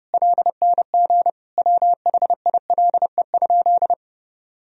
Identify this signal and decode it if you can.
Morse code, heard as a keyed tone, about 30 wpm, 700 Hz. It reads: LNG WHILE?
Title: Sentences from Top 300 Words and Abbreviations 30wpm